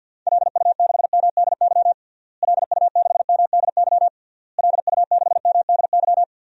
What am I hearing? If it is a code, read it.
Morse code, heard as a keyed tone, about 49 wpm, 700 Hz. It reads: LU6MDY LU6MDY LU6MDY
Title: DXCC-100 Call Signs - 6 Characters - Repeated 3x - 50wpm